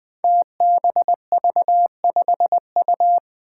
Morse code, heard as a keyed tone, about 20 wpm, 700 Hz. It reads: TBV5U
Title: Five Challenging Letter-Number Combinations 20wpm